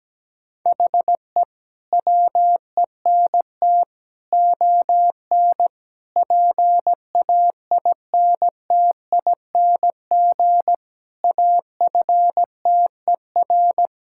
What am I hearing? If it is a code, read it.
Morse code, heard as a keyed tone, about 17 wpm, 700 Hz. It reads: HE WENT ON PAINTING AFTER
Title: Sentences from Top 1000 Words - Review Part 6 - 17wpm